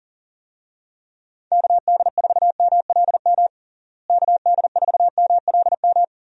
Morse code, heard as a keyed tone, about 38 wpm, 700 Hz. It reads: KD4MLM KD4MLM
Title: USA Call Signs - 6 Characters - Repeated 2x - 40wpm